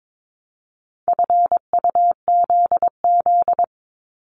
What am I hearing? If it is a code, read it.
Morse code, heard as a keyed tone, about 22 wpm, 700 Hz. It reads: FUZZ
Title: Multi-letter Phonemes 22wpm